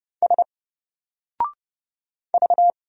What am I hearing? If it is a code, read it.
Morse code, heard as a keyed tone, about 30 wpm, 700 Hz. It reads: S V
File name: Single-Letter-Number-30wpm